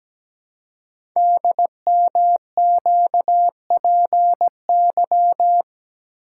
Morse code, heard as a keyed tone, about 17 wpm, 700 Hz. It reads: DMQPY
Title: Code Groups - 5 Letters - 17wpm